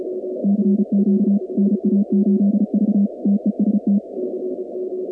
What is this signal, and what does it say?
Morse code, heard as a keyed tone, about 34 wpm, 200 Hz. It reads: CQ NA8V TEST
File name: contests / 2019-cw-wpx-kt5j-k5pi- -2330 0-46